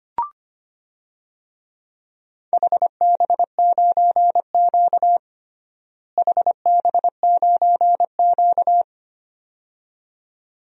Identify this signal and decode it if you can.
Morse code, heard as a keyed tone, about 25 wpm, 700 Hz. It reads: HB9Q HB9Q
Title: DXCC-100 Call Signs - 4 Characters - Repeated 2x - 25wpm